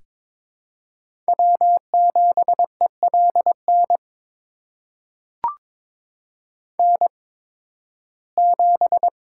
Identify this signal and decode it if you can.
Morse code, heard as a keyed tone, about 22 wpm, spaced as for 14 wpm, 700 Hz. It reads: W7ELN N7
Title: USA Call Signs - 5 Characters - 8x Character Spacing - 22wpm